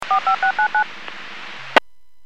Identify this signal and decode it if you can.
DTMF, heard as telephone touch tones, tones of about 100 ms. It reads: 45699